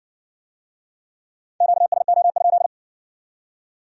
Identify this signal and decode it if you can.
Morse code, heard as a keyed tone, about 57 wpm, 700 Hz. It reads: QSO?